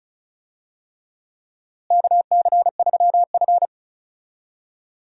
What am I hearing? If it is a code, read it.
Morse code, heard as a keyed tone, about 36 wpm, 700 Hz. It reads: KC3F
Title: USA Call Signs - 4 Characters - Repeated 3x - 35wpm